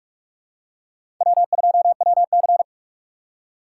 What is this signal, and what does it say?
Morse code, heard as a keyed tone, about 47 wpm, 700 Hz. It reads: W2WC
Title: USA Call Signs - 4 Characters - 2x Character Spacing - 45wpm